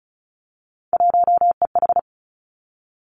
Morse code, heard as a keyed tone, about 36 wpm, 700 Hz. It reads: 1EH
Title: Three Challenging Letter-Number Combinations 35wpm